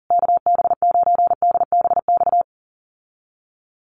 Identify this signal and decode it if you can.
Morse code, heard as a keyed tone, about 37 wpm, 700 Hz. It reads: KB9DBX
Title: USA Call Signs Repeated 3x 40wpm